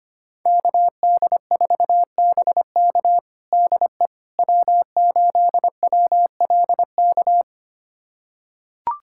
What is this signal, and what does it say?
Morse code, heard as a keyed tone, about 25 wpm, 700 Hz. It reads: KD4BK DE W8WLK
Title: USA Callsign DE Callsign 25wpm